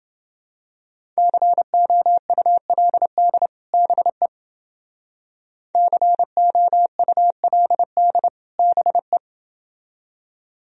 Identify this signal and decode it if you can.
Morse code, heard as a keyed tone, about 29 wpm, 700 Hz. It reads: COULD BE COULD BE